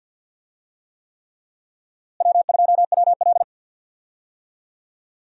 Morse code, heard as a keyed tone, about 51 wpm, 700 Hz. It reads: W2WL
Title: USA Call Signs - 4 Characters - Repeated 3x - 50wpm